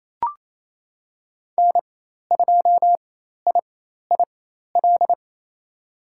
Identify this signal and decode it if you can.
Morse code, heard as a keyed tone, about 28 wpm, spaced as for 13 wpm, 700 Hz. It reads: N2IIL
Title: USA Call Signs - 5 Characters - 4x Character Spacing - 28wpm